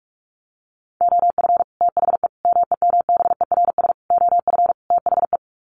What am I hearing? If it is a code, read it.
Morse code, heard as a keyed tone, about 44 wpm, 700 Hz. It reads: OF THE MEMBERS OF THE
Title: Sets of 5 Words Encore2 45wpm